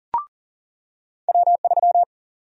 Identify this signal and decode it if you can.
Morse code, heard as a keyed tone, about 40 wpm, 700 Hz. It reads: W3
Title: USA Call Sign Prefixes - 40wpm